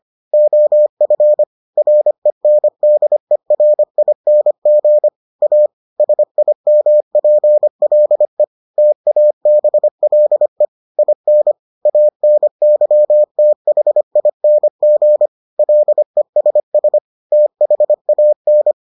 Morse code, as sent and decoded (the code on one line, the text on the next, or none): --- ..-. / .-. -. -.. .-. .. -. --. / .- / ... .. -- .--. .-.. / - .- -... .-.. / .. -. / .- -. -.-- - .... .. -. --. / .-.. ... ... / - .... .- -.
OF RENDERING A SIMPLE TABLE IN ANYTHING LESS THAN